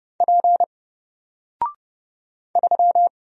Morse code, heard as a keyed tone, about 30 wpm, 700 Hz. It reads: P 3
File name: Single-Letter-Number-30wpm